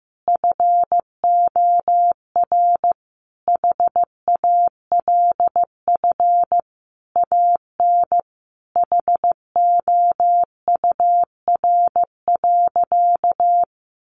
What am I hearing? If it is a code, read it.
Morse code, heard as a keyed tone, about 15 wpm, 700 Hz. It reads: FOR HALF AN HOUR.
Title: Sentences from Top 1100 Words 15wpm